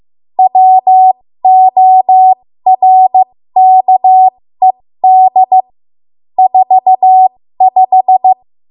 Morse code, heard as a keyed tone, about 15 wpm, spaced as for 13 wpm, 750 Hz. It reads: WORKED 45
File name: Morse / 13WPM